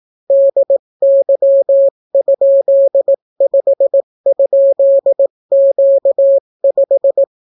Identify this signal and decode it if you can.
Morse code, heard as a keyed tone, about 18 wpm, 550 Hz. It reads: DY?5?Q5